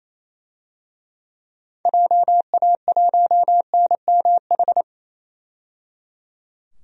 Morse code, heard as a keyed tone, about 28 wpm, 700 Hz. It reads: JA1NMH
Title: DXCC-100 Call Signs - 6 Characters - 28wpm